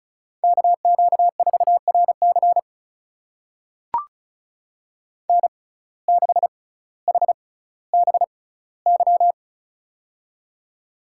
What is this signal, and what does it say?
Morse code, heard as a keyed tone, about 34 wpm, spaced as for 18 wpm, 700 Hz. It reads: KQ4RC N6HBY
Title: USA Call Signs - 5 Characters - 6x Character Spacing - 35wpm